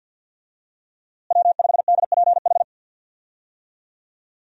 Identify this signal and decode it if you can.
Morse code, heard as a keyed tone, about 51 wpm, 700 Hz. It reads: W5DPH
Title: USA Call Signs - 5 Characters - Repeated 2x - 50wpm